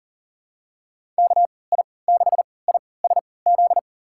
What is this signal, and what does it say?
Morse code, heard as a keyed tone, about 40 wpm, spaced as for 23 wpm, 700 Hz. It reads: KI6ISZ
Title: USA Call Signs - 6 Characters - 3x Character Spacing - 40wpm